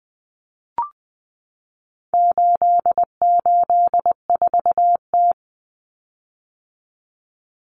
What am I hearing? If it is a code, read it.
Morse code, heard as a keyed tone, about 20 wpm, 700 Hz. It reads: 884T
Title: Common Cut Numbers - 4-Digit 20wpm